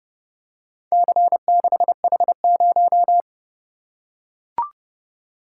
Morse code, heard as a keyed tone, about 31 wpm, 700 Hz. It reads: C6H0